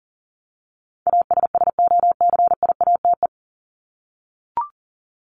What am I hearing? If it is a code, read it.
Morse code, heard as a keyed tone, about 40 wpm, 700 Hz. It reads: ASSOCIATE
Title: Top 1751-2000 Words 40wpm